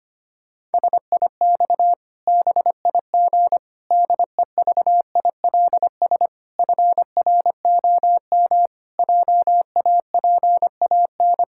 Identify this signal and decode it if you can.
Morse code, heard as a keyed tone, about 25 wpm, 700 Hz. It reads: SIX BIG DEVILS FROM JAPAN
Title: Pangram Challenge v2 - 25wpm